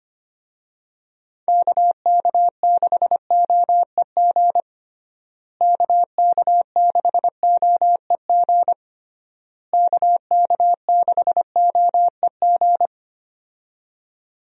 Morse code, heard as a keyed tone, about 25 wpm, 700 Hz. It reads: KK6OEG KK6OEG KK6OEG
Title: USA Call Signs - 6 Characters - Repeated 3x - 25wpm